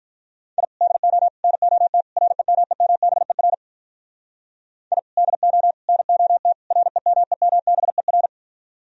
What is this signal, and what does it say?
Morse code, heard as a keyed tone, about 46 wpm, 700 Hz. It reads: I DO NOT REMEMBER I DO NOT REMEMBER